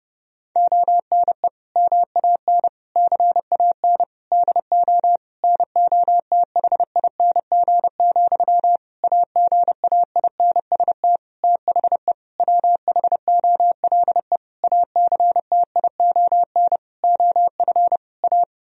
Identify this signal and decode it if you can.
Morse code, heard as a keyed tone, about 30 wpm, 700 Hz. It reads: ONE MAN CAN DO NOTHING, AGAINST THE WHOLE ACTION OF A
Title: Sentences from Top 1000 Words - Review Part 4 - 30wpm